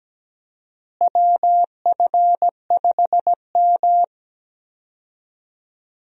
Morse code, heard as a keyed tone, about 17 wpm, 700 Hz. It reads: WF5M
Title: USA Call Signs - 4 Characters - Repeated 3x - 17wpm